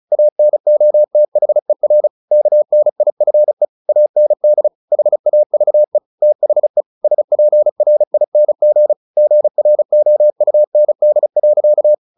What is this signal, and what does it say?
Morse code, heard as a keyed tone, about 34 wpm, 600 Hz. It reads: ANOTHER KNIFE AND HAVE THE SPRING GROUND.
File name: the old man and the sea 2 35wpm